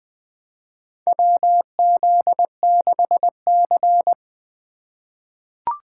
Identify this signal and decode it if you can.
Morse code, heard as a keyed tone, about 20 wpm, 700 Hz. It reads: WZ6C
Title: USA Call Signs - 4 Characters - 2x Character Spacing - 20wpm